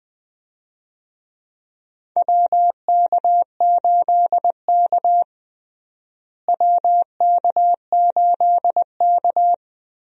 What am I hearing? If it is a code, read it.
Morse code, heard as a keyed tone, about 20 wpm, 700 Hz. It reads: WK8K WK8K
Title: USA Call Signs - 4 Characters - Repeated 2x - 20wpm